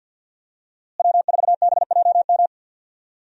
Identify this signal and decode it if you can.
Morse code, heard as a keyed tone, about 52 wpm, 700 Hz. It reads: W4BJM
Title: USA Call Signs - 5 Characters - Repeated 3x - 50wpm